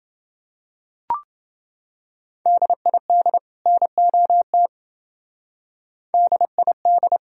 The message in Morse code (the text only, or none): DID NOT DID